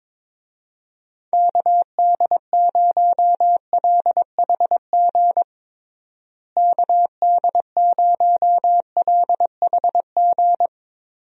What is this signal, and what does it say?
Morse code, heard as a keyed tone, about 22 wpm, 700 Hz. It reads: KD0LHG KD0LHG
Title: USA Call Signs - 6 Characters - Repeated 2x - 22wpm